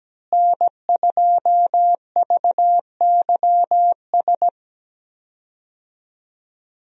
Morse code, heard as a keyed tone, about 17 wpm, 700 Hz. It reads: N2VYS
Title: USA Call Signs - 5 Characters - Repeated 2x - 17wpm